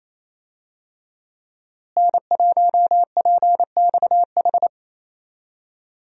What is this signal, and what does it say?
Morse code, heard as a keyed tone, about 28 wpm, 700 Hz. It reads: N1PXH
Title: USA Call Signs - 5 Characters - Repeated 3x - 28wpm